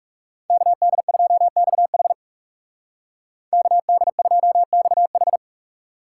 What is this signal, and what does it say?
Morse code, heard as a keyed tone, about 36 wpm, 700 Hz. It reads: KD2XH KD2XH
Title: Speed Racing - USA Call Signs 30wpm